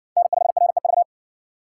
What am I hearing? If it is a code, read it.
Morse code, heard as a keyed tone, about 58 wpm, 700 Hz. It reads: N5REV